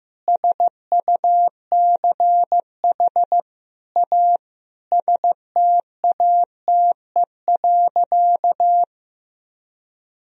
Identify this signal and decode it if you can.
Morse code, heard as a keyed tone, about 16 wpm, 700 Hz. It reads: SUCH A STATE.
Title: Sentences from Top 1000 Words - Review Part 2 - 15wpm